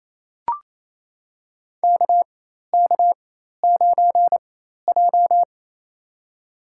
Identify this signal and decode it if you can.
Morse code, heard as a keyed tone, about 28 wpm, spaced as for 13 wpm, 700 Hz. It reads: KK9J
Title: USA Call Signs - 4 Characters - 4x Character Spacing - 28wpm